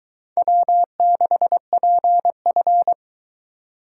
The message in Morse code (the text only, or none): W6PF